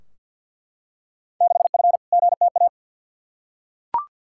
Morse code, heard as a keyed tone, about 47 wpm, 700 Hz. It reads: 6F GTA
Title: Field Day Exchanges Encore - 50wpm